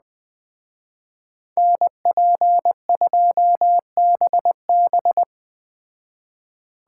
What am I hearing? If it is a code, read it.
Morse code, heard as a keyed tone, about 21 wpm, 700 Hz. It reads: NP2BB
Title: Speed Racing - USA 2x2 Call Signs 17wpm